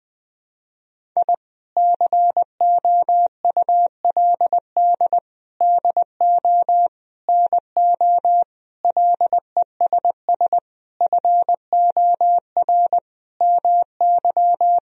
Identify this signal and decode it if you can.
Morse code, heard as a keyed tone, about 20 wpm, 700 Hz. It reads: I COULD DO NO LESS FOR MY
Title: Sentences from Top 1000 Words - Review Part 2 - 20wpm